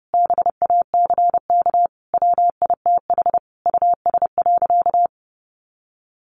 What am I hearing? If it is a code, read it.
Morse code, heard as a keyed tone, about 30 wpm, 700 Hz. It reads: BACK WITH US.
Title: A Sound Of Thunder 30wpm